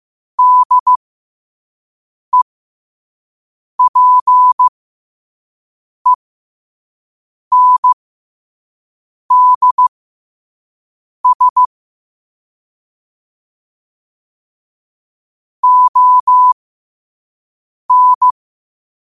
Morse code, heard as a keyed tone, about 15 wpm, spaced as for 5 wpm, 1000 Hz. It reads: DEPENDS ON
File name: qotdfiles / 5wpm